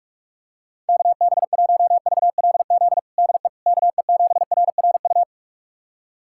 Morse code, heard as a keyed tone, about 43 wpm, 700 Hz. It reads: KB1VLZ DE KE7RRU
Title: USA Callsign DE Callsign 45wpm